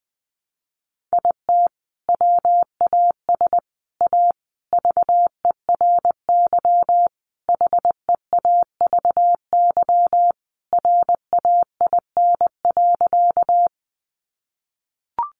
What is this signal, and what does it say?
Morse code, heard as a keyed tone, about 20 wpm, 700 Hz. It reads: IT WAS A VERY HEAVY RAIN.
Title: Sentences from Top 1100 Words 20wpm